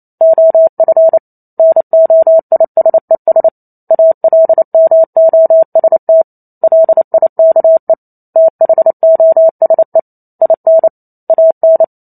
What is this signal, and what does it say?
Morse code, heard as a keyed tone, about 28 wpm, 650 Hz. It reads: OF NOISES ALMOST LIKE THOSE IN AN